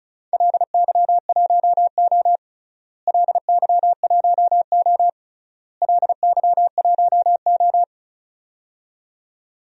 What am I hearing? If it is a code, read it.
Morse code, heard as a keyed tone, about 34 wpm, 700 Hz. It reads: LY1O LY1O LY1O
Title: DXCC-100 Call Signs - 4 Characters - Repeated 3x - 35wpm